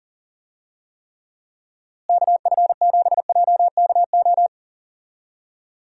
Morse code, heard as a keyed tone, about 37 wpm, 700 Hz. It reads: KF7JKO
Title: USA Call Signs - 6 Characters - Repeated 2x - 40wpm